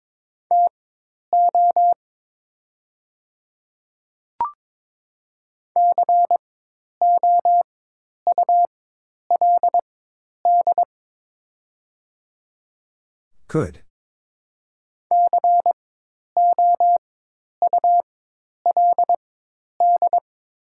Morse code, heard as a keyed tone, about 22 wpm, spaced as for 10 wpm, 700 Hz. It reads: TO COULD COULD
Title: 4x Character Spacing - Top 100 Words - 22wpm